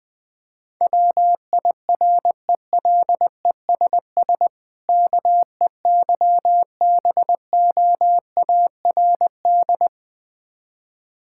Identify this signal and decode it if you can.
Morse code, heard as a keyed tone, about 20 wpm, 690 Hz. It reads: WIRELESS KEYBOARD
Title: Difficult Words with Hints - 20wpm